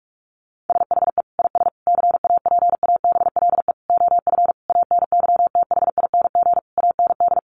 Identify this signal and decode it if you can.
Morse code, heard as a keyed tone, about 44 wpm, 700 Hz. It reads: SHE IS CAPABLE OF ANYTHING AND
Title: Sentences from Top 1200 Words 45wpm